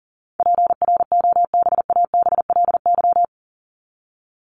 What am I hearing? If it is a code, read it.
Morse code, heard as a keyed tone, about 40 wpm, 700 Hz. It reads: PROBABLY
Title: Top 501-750 Words 40wpm